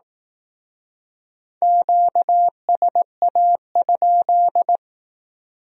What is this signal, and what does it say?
Morse code, heard as a keyed tone, about 18 wpm, 700 Hz. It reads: QSA?
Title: Speed-Racing Q-Codes - 15wpm